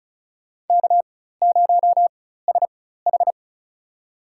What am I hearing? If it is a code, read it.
Morse code, heard as a keyed tone, about 34 wpm, spaced as for 16 wpm, 690 Hz. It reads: K0SH
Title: USA Call Signs - 4x Character Spacing - 35wpm